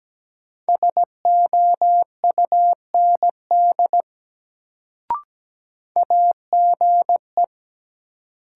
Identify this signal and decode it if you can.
Morse code, heard as a keyed tone, about 17 wpm, 700 Hz. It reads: SOUND AGE
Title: Sets of 3 Words - New Word Drill - 17wpm